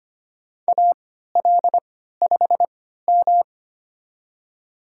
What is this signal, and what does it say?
Morse code, heard as a keyed tone, about 25 wpm, spaced as for 14 wpm, 700 Hz. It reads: AL5M
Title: USA Call Signs - 4 Characters - 3x Character Spacing - 25wpm